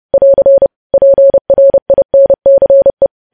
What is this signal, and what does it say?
Morse code, heard as a keyed tone, about 30 wpm, 550 Hz. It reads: <AR> PRINCE
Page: news headlines in morse code at 30 wpm